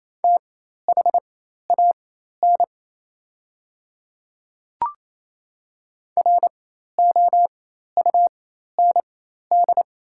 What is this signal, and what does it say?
Morse code, heard as a keyed tone, about 27 wpm, spaced as for 13 wpm, 700 Hz. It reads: THAN ROUND
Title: 4x Character Spacing - Top 200 Words - 28wpm